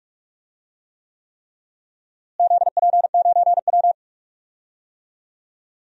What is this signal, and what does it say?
Morse code, heard as a keyed tone, about 46 wpm, 700 Hz. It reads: ZP9W
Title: DXCC-100 Call Signs - 4 Characters - Repeated 3x - 45wpm